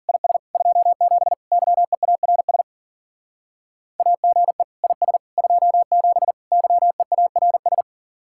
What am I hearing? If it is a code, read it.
Morse code, heard as a keyed tone, about 49 wpm, 700 Hz. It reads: IS 27 YEARS AGE IS 27 YEARS
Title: Speed-Racing Common QSO Phrases Encore - 40wpm